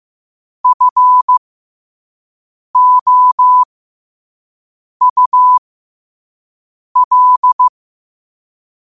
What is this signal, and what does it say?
Morse code, heard as a keyed tone, about 15 wpm, spaced as for 5 wpm, 1000 Hz. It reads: FOUL
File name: qotdfiles / 5wpm